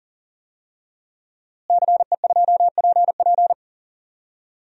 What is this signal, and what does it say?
Morse code, heard as a keyed tone, about 40 wpm, 700 Hz. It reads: CE2PP
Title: DXCC-100 Call Signs - 5 Characters - Repeated 3x - 40wpm